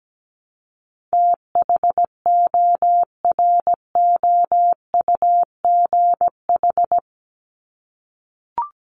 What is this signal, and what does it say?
Morse code, heard as a keyed tone, about 17 wpm, 700 Hz. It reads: THOROUGH
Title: LY Suffix - Top 100 Words with Base Words 17wpm